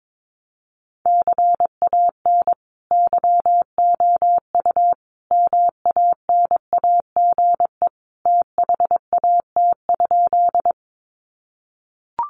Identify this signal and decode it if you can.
Morse code, heard as a keyed tone, about 22 wpm, 700 Hz. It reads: CAN YOU MANAGE THAT?
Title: Sentences from Top 1400 Words 22wpm